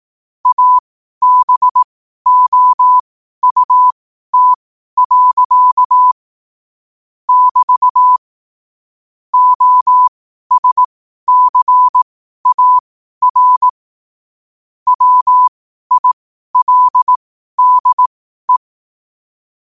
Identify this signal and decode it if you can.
Morse code, heard as a keyed tone, about 18 wpm, spaced as for 12 wpm, 1000 Hz. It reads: ABOUT. <BT> OSCAR WILDE
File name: qotdfiles / 12wpm